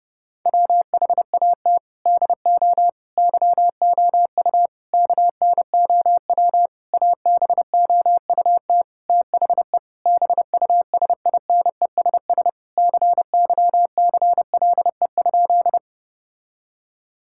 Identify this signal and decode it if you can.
Morse code, heard as a keyed tone, about 30 wpm, 700 Hz. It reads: WHAT DO YOU KNOW ABOUT THE BUSINESS CYCLE?
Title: Sentences from Top 1000 Words - Review Part 6 - 30wpm